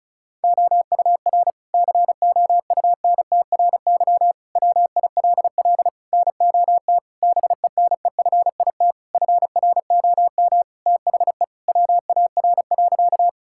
Morse code, heard as a keyed tone, about 36 wpm, 700 Hz. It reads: OUR COUNTRY WILL NOT BENEFIT FROM THE WAR.
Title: Sentences from Top 1000 Words - Review Part 6 - 35wpm